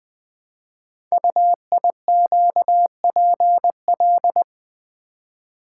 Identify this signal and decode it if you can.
Morse code, heard as a keyed tone, about 20 wpm, 700 Hz. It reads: UIQPL
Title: Code Groups - 5 Letters - 20wpm